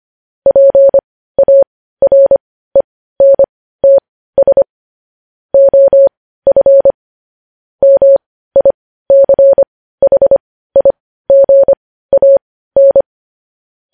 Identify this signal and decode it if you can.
Morse code, heard as a keyed tone, about 25 wpm, spaced as for 15 wpm, 550 Hz. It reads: PARENTS OF MICHIGAN
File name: podcast / News headlines-1x-25x15